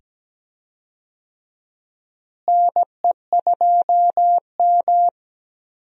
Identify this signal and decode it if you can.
Morse code, heard as a keyed tone, about 17 wpm, 700 Hz. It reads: NE2M
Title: USA Call Signs - 4 Characters - Repeated 2x - 17wpm